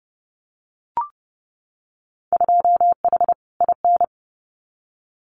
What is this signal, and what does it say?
Morse code, heard as a keyed tone, about 30 wpm, 700 Hz. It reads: 2H IN
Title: Winter Field Day Exchanges Encore 30wpm